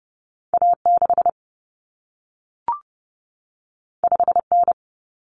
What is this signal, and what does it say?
Morse code, heard as a keyed tone, about 30 wpm, 700 Hz. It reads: A6 5N
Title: Common Cut Numbers - 2-Digit 30wpm